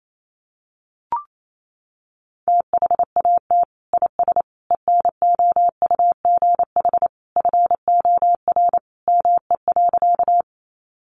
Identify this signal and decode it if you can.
Morse code, heard as a keyed tone, about 28 wpm, 700 Hz. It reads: THAT IS ENOUGH FOR ME.
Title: Sentences from Top 400 Words 28wpm